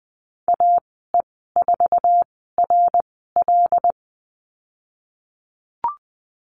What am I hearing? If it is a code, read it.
Morse code, heard as a keyed tone, about 19 wpm, spaced as for 14 wpm, 700 Hz. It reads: AE4RL
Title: 076 Lesson 13g - Letter L - Call Signs 20wpm